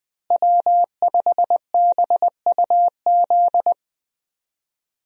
Morse code, heard as a keyed tone, about 20 wpm, 700 Hz. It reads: W5BUZ
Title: USA Call Signs - 5 Characters - Repeated 2x - 20wpm